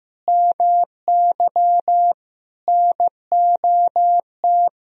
Morse code, heard as a keyed tone, about 16 wpm, 700 Hz. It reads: MY NOT
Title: Sentences from Top 1000 Words - Review Part 3 - 15wpm